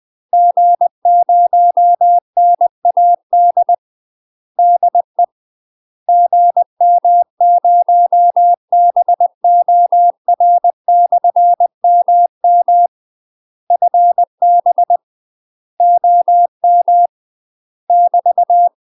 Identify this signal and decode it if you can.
Morse code, heard as a keyed tone, about 20 wpm, 700 Hz. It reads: G0NAD DE GM0BOR/MM FB OM <BT>
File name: UK MASTER QSOs TRAINING / 20WPM / UK MASTER QSOs 0025